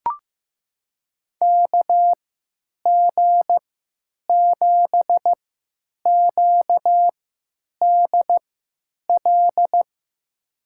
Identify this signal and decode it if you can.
Morse code, heard as a keyed tone, about 15 wpm, spaced as for 9 wpm, 700 Hz. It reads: KG7QDL
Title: USA Call Signs - 6 Characters - 3x Character Spacing - 15wpm